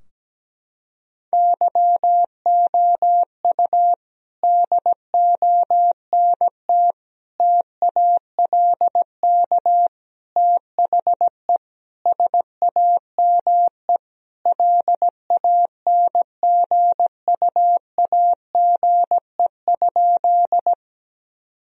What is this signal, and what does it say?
Morse code, heard as a keyed tone, about 17 wpm, 700 Hz. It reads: YOU DONT TALK THE SAME LANGUAGE?
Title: Sentences from Top 1000 Words - Review Part 3 - 17wpm